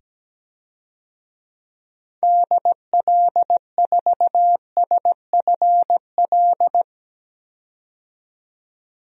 Morse code, heard as a keyed tone, about 17 wpm, 700 Hz. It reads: DL4SFL
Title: DXCC-100 Call Signs - 6 Characters - 17wpm